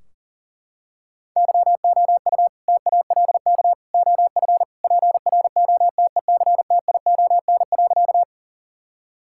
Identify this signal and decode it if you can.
Morse code, heard as a keyed tone, about 40 wpm, 700 Hz. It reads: YOU TALK OF PROTECTION.
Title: Sentences from Top 1600 Words 40wpm